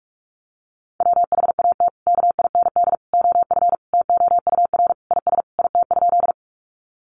Morse code, heard as a keyed tone, about 44 wpm, 700 Hz. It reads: WHAT KIND OF TOUR IS IT?
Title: Sentences from Top 1400 Words 45wpm